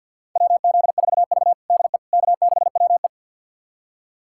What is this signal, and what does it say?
Morse code, heard as a keyed tone, about 51 wpm, 700 Hz. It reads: WZ4V DE K6WE
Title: USA Callsign DE Callsign - 4 Characters - 50wpm